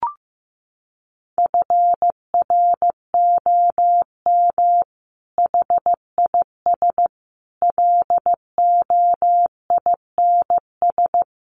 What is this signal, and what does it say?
Morse code, heard as a keyed tone, about 15 wpm, 700 Hz. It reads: FROM HIS LOINS
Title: A Sound Of Thunder 15wpm